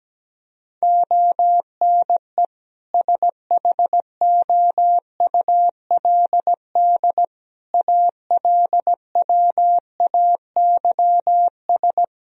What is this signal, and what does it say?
Morse code, heard as a keyed tone, about 17 wpm, 700 Hz. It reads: ONE SHOULD ALWAYS
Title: Sentences from Top 1000 Words - Review Part 4 - 17wpm